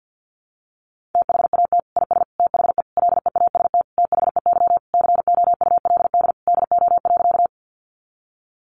Morse code, heard as a keyed tone, about 48 wpm, 700 Hz. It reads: THAT IS THE LEAST THEY COULD DO.
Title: Sentences from Top 700 Words 50wpm